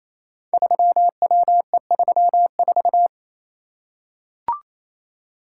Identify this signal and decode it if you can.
Morse code, heard as a keyed tone, about 28 wpm, 700 Hz. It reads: 3WE34
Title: Code Groups - 5 Letter:Numbers - 28wpm